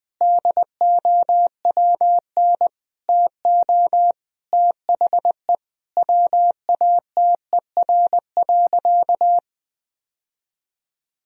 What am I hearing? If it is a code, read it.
Morse code, heard as a keyed tone, about 20 wpm, 700 Hz. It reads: DOWN TO THE WATER.
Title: Sentences from Top 1000 Words - Review Part 4 - 20wpm